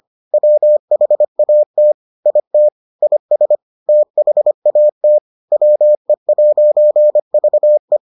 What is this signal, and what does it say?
Morse code, heard as a keyed tone, about 25 wpm, 600 Hz. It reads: WHAT IT IS THAT WE'VE